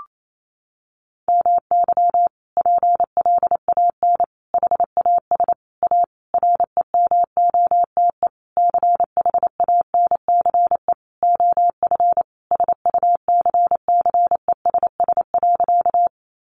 Morse code, heard as a keyed tone, about 28 wpm, 700 Hz. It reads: MY PLAN HAS A REMOTE CHANCE OF SUCCESS.
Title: Sentences from Top 1300 Words 28wpm